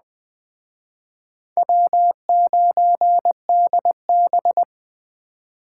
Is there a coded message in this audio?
Yes: Morse code from 1.6 to 4.6 s, heard as a keyed tone, about 21 wpm, 700 Hz. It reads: W9DB